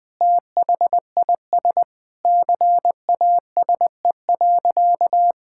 Morse code, heard as a keyed tone, about 20 wpm, 700 Hz. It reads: THIS CASE.
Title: All-Age Appropriate Jokes - 20wpm